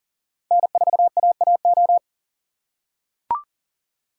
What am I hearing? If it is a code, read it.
Morse code, heard as a keyed tone, about 40 wpm, 700 Hz. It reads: N4AAO
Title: USA Call Signs - 5 Characters - 4x Character Spacing - 40wpm